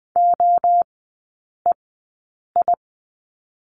Morse code, heard as a keyed tone, about 20 wpm, spaced as for 8 wpm, 700 Hz. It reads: OEI
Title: 053 Lesson 10e - Letter r - Three Characters 20wpm